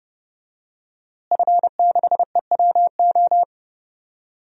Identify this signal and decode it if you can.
Morse code, heard as a keyed tone, about 30 wpm, 700 Hz. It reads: F6EWO
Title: DXCC-100 Call Signs - 5 Characters - Repeated 3x - 30wpm